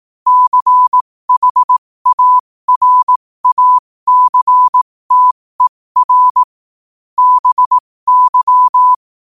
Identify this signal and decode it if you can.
Morse code, heard as a keyed tone, about 18 wpm, spaced as for 15 wpm, 1000 Hz. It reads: CHARACTER BY
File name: qotdfiles / 15wpm